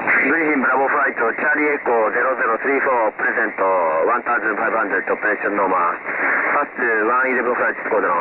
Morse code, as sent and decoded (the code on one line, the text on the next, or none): none